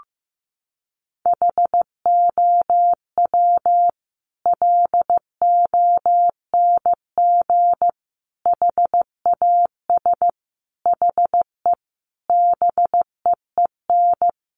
Morse code, heard as a keyed tone, about 15 wpm, 700 Hz. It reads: HOW LONG HAS HE BEEN
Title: Sentences from Top 100 Words 15wpm